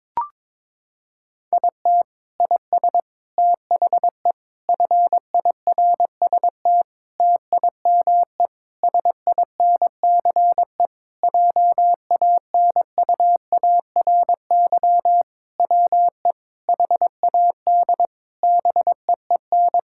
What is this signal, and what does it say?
Morse code, heard as a keyed tone, about 21 wpm, 700 Hz. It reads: IT IS THE FIRST TIME SINCE JANUARY WE HAD BEEN
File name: practice / 22wpm / Sentences with 4-Word Phrases - 22wpm